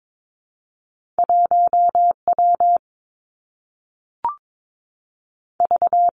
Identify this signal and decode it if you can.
Morse code, heard as a keyed tone, about 22 wpm, 700 Hz. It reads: 1 W V
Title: Two Challenging Letter-Number Combinations 22wpm